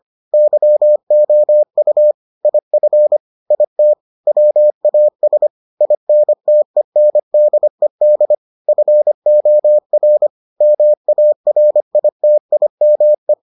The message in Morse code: YOU IF IT WAS INTENDED FOR MARITIME

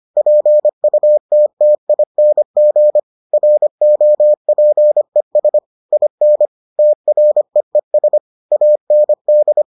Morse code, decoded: PUTTING ROPES IN TREES AND

